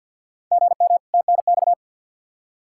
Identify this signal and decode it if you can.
Morse code, heard as a keyed tone, about 51 wpm, 700 Hz. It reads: GM TNX
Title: Practice QSO - Exchange 1 - Beginning - 50wpm